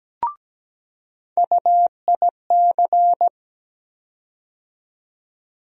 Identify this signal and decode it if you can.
Morse code, heard as a keyed tone, about 17 wpm, 700 Hz. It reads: UIC
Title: USA Call Sign Postfixes - 3 Letters 17wpm